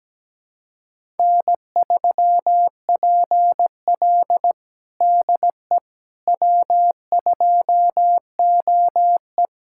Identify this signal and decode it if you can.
Morse code, heard as a keyed tone, about 17 wpm, 700 Hz. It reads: N3PL DE W2OE
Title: USA Callsign DE Callsign - 4 Characters - 17wpm